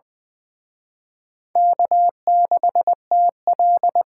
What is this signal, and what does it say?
Morse code, heard as a keyed tone, about 24 wpm, 700 Hz. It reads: K6TL
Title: Speed Racing - USA 1x2 Call Signs 20wpm